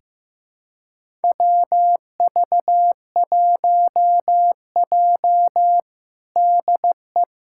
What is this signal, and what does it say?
Morse code, heard as a keyed tone, about 15 wpm, 700 Hz. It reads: WV1J DE
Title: USA Callsign DE Callsign - 4 Characters - 15wpm